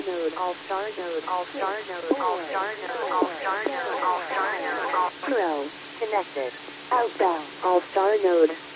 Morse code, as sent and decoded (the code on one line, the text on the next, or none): none